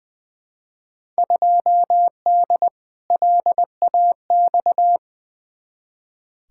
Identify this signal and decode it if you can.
Morse code, heard as a keyed tone, about 20 wpm, 700 Hz. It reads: 2D LAX
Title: Field Day Exchanges - 20wpm